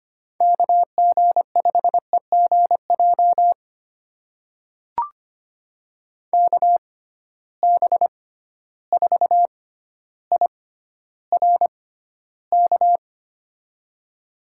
Morse code, heard as a keyed tone, about 25 wpm, spaced as for 13 wpm, 700 Hz. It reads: KG5EGJ KB4IRK
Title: USA Call Signs - 6 Characters - 6x Character Spacing - 25wpm